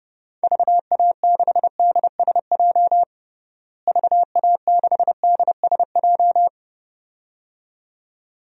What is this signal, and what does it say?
Morse code, heard as a keyed tone, about 31 wpm, 700 Hz. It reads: VA6DSJ VA6DSJ
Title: DXCC-100 Call Signs - 6 Characters - Repeated 2x - 30wpm